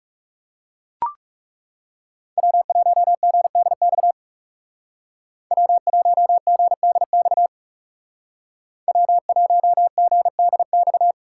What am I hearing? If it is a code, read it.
Morse code, heard as a keyed tone, about 36 wpm, 700 Hz. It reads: W1GDX W1GDX W1GDX
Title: Speed Racing - USA 1x3 Call Signs 30wpm